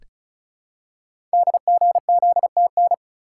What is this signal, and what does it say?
Morse code, heard as a keyed tone, about 35 wpm, 690 Hz. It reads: DGZTN
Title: Code Groups - 5 Letter:Numbers - 35wpm